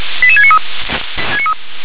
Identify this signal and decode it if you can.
Five-tone selective call (ZVEI), heard as quick runs of five tones, tones of about 70 ms. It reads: 88693